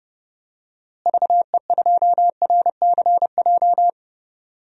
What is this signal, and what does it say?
Morse code, heard as a keyed tone, about 30 wpm, 700 Hz. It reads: VE2RCJ
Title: DXCC-100 Call Signs - 6 Characters - Repeated 2x - 30wpm